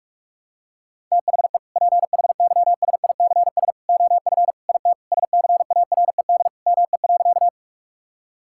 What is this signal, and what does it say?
Morse code, heard as a keyed tone, about 42 wpm, 690 Hz. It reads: THE PHYSICS OF IT SCARED ME.
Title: Sentences from Top 1000 Words - Review Part 5 - 45wpm